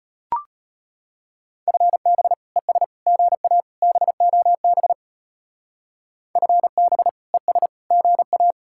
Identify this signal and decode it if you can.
Morse code, heard as a keyed tone, about 31 wpm, 700 Hz. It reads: FB ES GA BOB FB ES GA
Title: Speed-Racing Common QSO Phrases - 25wpm